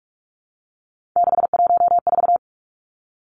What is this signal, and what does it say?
Morse code, heard as a keyed tone, about 46 wpm, 700 Hz. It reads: 614
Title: Three Challenging Letter-Number Combinations 45wpm